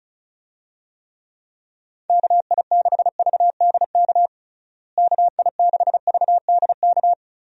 Morse code, heard as a keyed tone, about 34 wpm, 700 Hz. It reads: KI6VDK KI6VDK
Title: USA Call Signs - 6 Characters - Repeated 2x - 35wpm